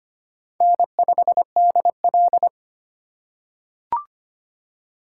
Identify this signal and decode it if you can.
Morse code, heard as a keyed tone, about 25 wpm, 700 Hz. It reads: N5DL
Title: USA Call Signs - 3x Character Spacing - 25wpm